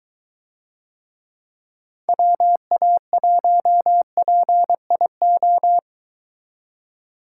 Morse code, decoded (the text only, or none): WA1PIO